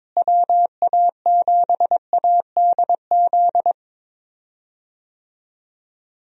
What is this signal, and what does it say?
Morse code, heard as a keyed tone, about 22 wpm, 700 Hz. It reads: WA7ADZ
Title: USA Call Signs - 6 Characters - Repeated 2x - 22wpm